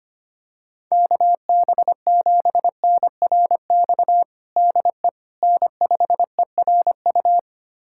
Morse code, heard as a keyed tone, about 25 wpm, 700 Hz. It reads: KB7NRX DE N5ERU
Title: USA Callsign DE Callsign 25wpm